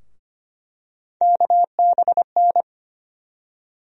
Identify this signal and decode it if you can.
Morse code, heard as a keyed tone, about 25 wpm, 700 Hz. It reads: KBN
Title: USA Call Sign Postfixes - 3 Letters 25wpm